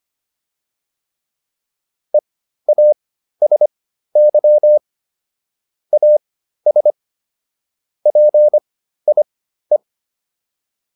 Morse code, heard as a keyed tone, about 25 wpm, spaced as for 13 wpm, 600 Hz. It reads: EASY AS PIE